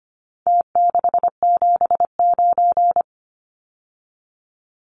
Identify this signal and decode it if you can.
Morse code, heard as a keyed tone, about 25 wpm, 700 Hz. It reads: T679
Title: Numbers - Contest Style - 4-Digit 25wpm